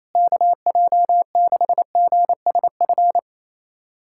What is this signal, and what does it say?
Morse code, heard as a keyed tone, about 28 wpm, 700 Hz. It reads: KJ6GSF
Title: USA Callsign DE Callsign - 6 Characters - 28wpm